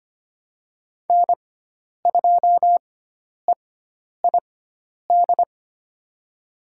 Morse code, heard as a keyed tone, about 25 wpm, spaced as for 10 wpm, 700 Hz. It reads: N2EID
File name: Lesson 16g - Letter c - Call Signs 25wpm